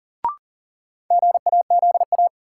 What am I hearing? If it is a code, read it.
Morse code, heard as a keyed tone, about 40 wpm, 700 Hz. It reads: GAZA
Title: Sets of 5 Words - New Word Drill - 40wpm